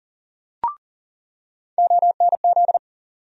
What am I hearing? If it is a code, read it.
Morse code, heard as a keyed tone, about 40 wpm, 700 Hz. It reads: ONZ
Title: USA Call Sign Postfixes - 3 Letters 40wpm